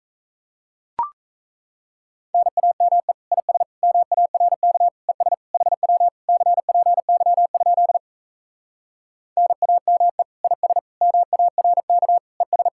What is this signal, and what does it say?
Morse code, heard as a keyed tone, about 37 wpm, 700 Hz. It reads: NAME IS MARK ES HW CPY? NAME IS MARK ES